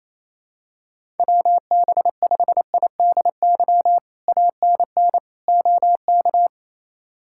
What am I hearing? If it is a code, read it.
Morse code, heard as a keyed tone, about 27 wpm, 700 Hz. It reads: WB5IDY ANN OK
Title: K1USN SST QSO Practice - 28wpm